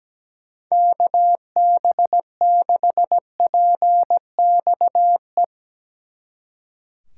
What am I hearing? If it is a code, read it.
Morse code, heard as a keyed tone, about 18 wpm, 700 Hz. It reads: KB6PXE
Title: Speed Racing - USA Call Signs 17wpm